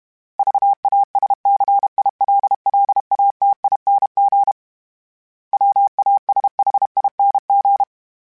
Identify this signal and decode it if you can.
Morse code, heard as a keyed tone, about 32 wpm, 800 Hz. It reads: VASCILLATING WASHING
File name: Special INGs-mp3s / Special INGs-mp3s / Sngl Ing @30